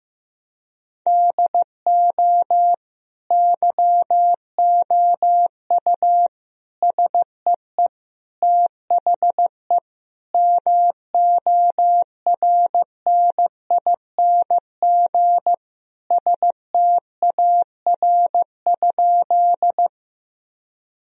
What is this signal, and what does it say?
Morse code, heard as a keyed tone, about 15 wpm, 700 Hz. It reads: DO YOU SEE THE MORNING STAR?
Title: Sentences from Top 1000 Words - Review Part 2 - 15wpm